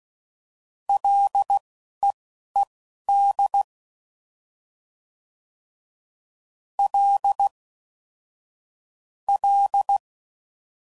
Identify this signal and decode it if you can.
Morse code, heard as a keyed tone, about 16 wpm, spaced as for 11 wpm, 800 Hz. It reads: LEED L L
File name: Prefixes / FLs & THs / L Prefixes